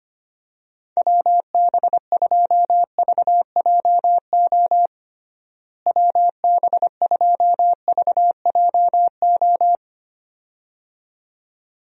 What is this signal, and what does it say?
Morse code, heard as a keyed tone, about 25 wpm, 700 Hz. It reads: WB2VJO WB2VJO
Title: USA Call Signs - 6 Characters - Repeated 2x - 25wpm